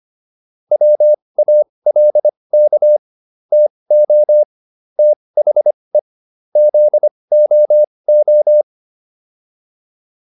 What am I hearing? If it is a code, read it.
Morse code, heard as a keyed tone, about 25 wpm, spaced as for 20 wpm, 600 Hz. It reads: WALK TO THE ZOO